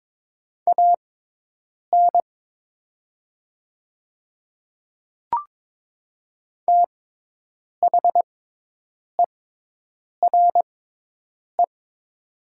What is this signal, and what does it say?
Morse code, heard as a keyed tone, about 22 wpm, spaced as for 8 wpm, 690 Hz. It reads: AN THERE
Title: 6x Character Spacing - Top 100 Words - 22wpm